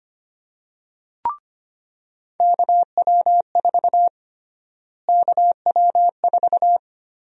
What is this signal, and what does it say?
Morse code, heard as a keyed tone, about 25 wpm, 700 Hz. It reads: KW4 KW4
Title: USA Call Sign Prefixes - Repeated 2x - 25wpm